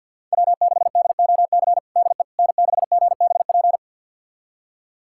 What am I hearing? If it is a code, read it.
Morse code, heard as a keyed tone, about 51 wpm, 700 Hz. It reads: W6DOC DE N6GBP
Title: USA Callsign DE Callsign - 5 Characters - 50wpm